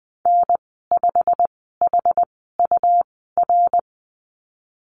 Morse code, heard as a keyed tone, about 19 wpm, spaced as for 14 wpm, 700 Hz. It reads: N5HUR
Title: 154 Lesson 25g - Letter p - Call Signs 20wpm